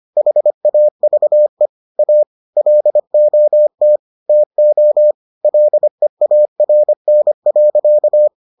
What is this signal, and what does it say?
Morse code, heard as a keyed tone, about 25 wpm, 600 Hz. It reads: HAVE A LOT TO LEARN.